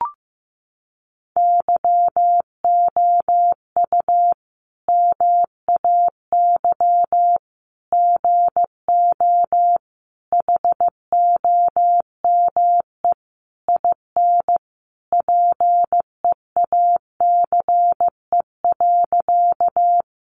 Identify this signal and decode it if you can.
Morse code, heard as a keyed tone, about 15 wpm, 700 Hz. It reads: YOU MAY GO HOME IN PEACE.
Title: Sentences from Top 1500 Words 15wpm